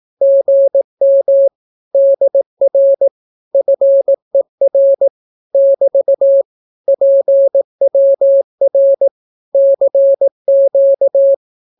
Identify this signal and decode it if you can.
Morse code, heard as a keyed tone, about 18 wpm, 550 Hz. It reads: GM DR FER <BT> PWR CQ
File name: naukapodstaw / skroty / skroty